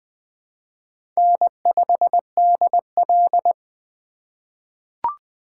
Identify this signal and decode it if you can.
Morse code, heard as a keyed tone, about 20 wpm, 700 Hz. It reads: N5DL
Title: USA Call Signs - 3x Character Spacing - 20wpm